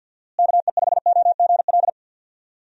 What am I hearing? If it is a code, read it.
Morse code, heard as a keyed tone, about 52 wpm, 700 Hz. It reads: KE5OGL
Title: USA Call Signs - 6 Characters - Repeated 2x - 50wpm